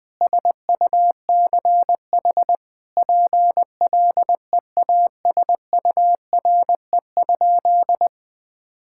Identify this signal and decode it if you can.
Morse code, heard as a keyed tone, about 20 wpm, 700 Hz. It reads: SUCH PLEASURE?
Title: Sentences from Top 1600 Words 20wpm